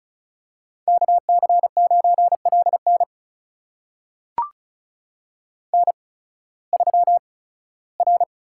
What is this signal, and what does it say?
Morse code, heard as a keyed tone, about 36 wpm, spaced as for 19 wpm, 700 Hz. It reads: KC9LN N3R
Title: USA Call Signs - 5 Characters - 8x Character Spacing - 35wpm